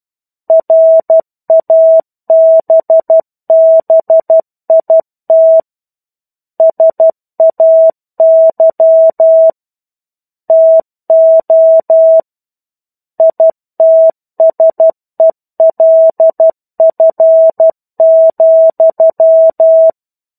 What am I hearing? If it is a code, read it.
Morse code, heard as a keyed tone, about 12 wpm, 650 Hz. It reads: RABBIT SAY TO ITSELF,
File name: 01-Chapter-01 12 WPM